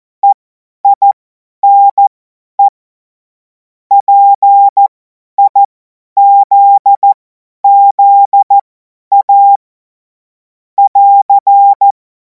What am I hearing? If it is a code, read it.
Morse code, heard as a keyed tone, about 14 wpm, spaced as for 10 wpm, 800 Hz. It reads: EINE PIZZA <AR>
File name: Morsekurs / Witze / Witz 30. Juni 2026 50 F70